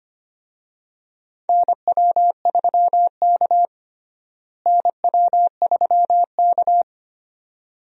Morse code, heard as a keyed tone, about 25 wpm, 700 Hz. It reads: NW3K NW3K
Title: USA Call Signs - 4 Characters - Repeated 2x - 25wpm